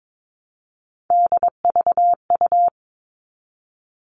Morse code, heard as a keyed tone, about 22 wpm, 700 Hz. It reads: DVU
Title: All Cut Numbers - 3-Digit 22wpm